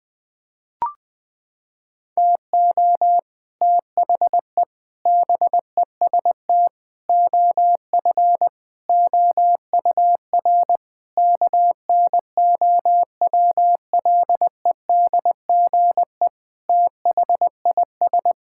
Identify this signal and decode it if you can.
Morse code, heard as a keyed tone, about 20 wpm, 700 Hz. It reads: TO THE BEST OF OUR KNOWLEDGE THIS
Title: Sentences with 8-Word Phrases - 20wpm